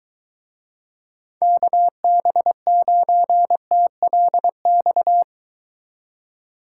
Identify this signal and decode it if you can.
Morse code, heard as a keyed tone, about 23 wpm, 700 Hz. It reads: KB9TLX